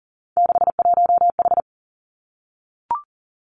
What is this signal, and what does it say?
Morse code, heard as a keyed tone, about 40 wpm, 700 Hz. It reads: 61H
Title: Three Challenging Letter-Number Combinations 40wpm